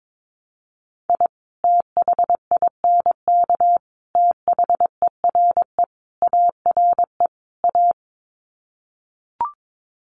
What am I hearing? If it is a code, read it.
Morse code, heard as a keyed tone, about 22 wpm, 700 Hz. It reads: I THINK THERE ARE A
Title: Sets of 5 Words Encore2 22wpm